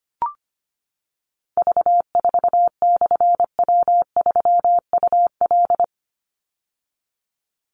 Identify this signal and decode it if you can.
Morse code, heard as a keyed tone, about 25 wpm, 700 Hz. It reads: V4/W3UL
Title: Call Signs - 25wpm